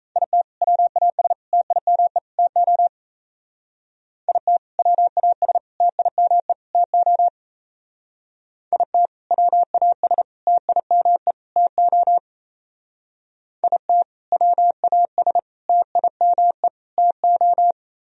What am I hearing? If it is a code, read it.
Morse code, heard as a keyed tone, about 34 wpm, 700 Hz. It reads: IT WAS TIME TO IT WAS TIME TO IT WAS TIME TO IT WAS TIME TO
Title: Speed-Racing - Sets of 4 Words 28wpm